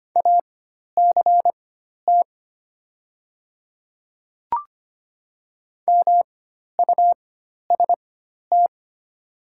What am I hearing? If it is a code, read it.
Morse code, heard as a keyed tone, about 25 wpm, spaced as for 12 wpm, 700 Hz. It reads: ACT MUST
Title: 4x Character Spacing - Top 200 Words - 25wpm